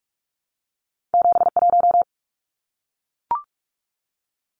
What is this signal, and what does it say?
Morse code, heard as a keyed tone, about 46 wpm, 700 Hz. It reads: 71
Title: Numbers - 2-Digit 45wpm